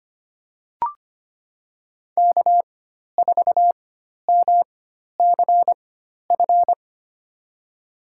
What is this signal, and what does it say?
Morse code, heard as a keyed tone, about 25 wpm, spaced as for 12 wpm, 700 Hz. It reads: K4MCF
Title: USA Call Signs - 5 Characters - 4x Character Spacing - 25wpm